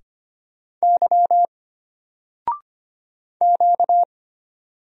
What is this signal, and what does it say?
Morse code, heard as a keyed tone, about 25 wpm, 700 Hz. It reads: Y Q
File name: Single-Letter-Number-25wpm